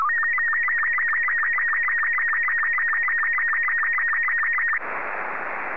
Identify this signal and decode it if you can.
SSTV (slow-scan television), Robot 36.
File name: sstv